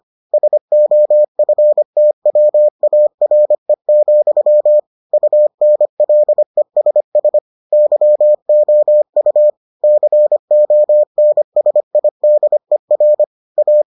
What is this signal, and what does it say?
Morse code, heard as a keyed tone, about 25 wpm, 600 Hz. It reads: SOFTWARE, UNLESS YOU CONSIDER A